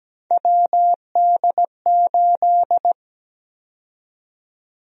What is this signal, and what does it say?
Morse code, heard as a keyed tone, about 17 wpm, 700 Hz. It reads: WD8
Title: USA Call Sign Prefixes - 17wpm